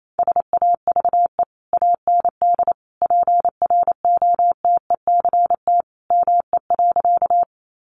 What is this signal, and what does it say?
Morse code, heard as a keyed tone, about 28 wpm, 700 Hz. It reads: SAVE AND PROTECT ME.
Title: Sentences from Top 1100 Words 28wpm